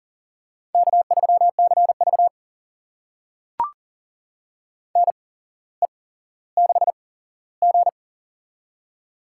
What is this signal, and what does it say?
Morse code, heard as a keyed tone, about 44 wpm, spaced as for 18 wpm, 700 Hz. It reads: K3CV NE6G
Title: USA Call Signs - 4 Characters - 8x Character Spacing - 40wpm